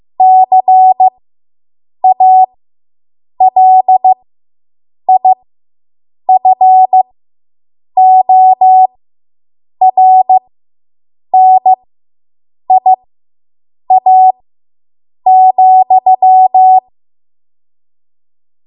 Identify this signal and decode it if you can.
Morse code, heard as a keyed tone, about 15 wpm, spaced as for 7 wpm, 750 Hz. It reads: CALIFORNIA,